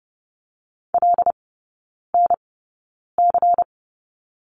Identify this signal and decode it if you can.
Morse code, heard as a keyed tone, about 30 wpm, spaced as for 9 wpm, 700 Hz. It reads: LNC
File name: Lesson 18e - Letter m - Three Characters 30wpm